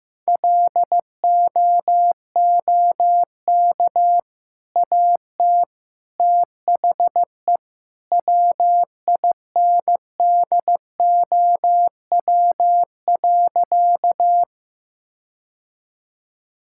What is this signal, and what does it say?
Morse code, heard as a keyed tone, about 15 wpm, 700 Hz. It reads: LOOK AT THE WINDOW.
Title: Sentences from Top 1600 Words 15wpm